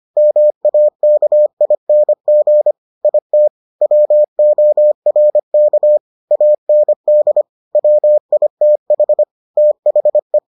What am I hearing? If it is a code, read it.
Morse code, heard as a keyed tone, about 25 wpm, 600 Hz. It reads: MAKING IT WORK AND WITH THE